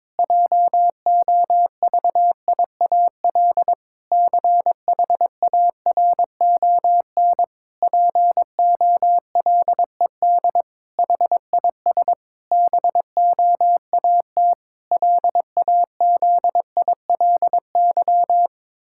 Morse code, heard as a keyed tone, about 21 wpm, 700 Hz. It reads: JOVIAL CHARON POLED HIS BOAT LAZILY
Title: Pangram Challenge v2 - 22wpm